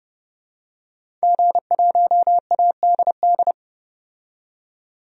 Morse code, heard as a keyed tone, about 29 wpm, 700 Hz. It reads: G1ADD
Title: DXCC-100 Call Signs - 5 Characters - 30wpm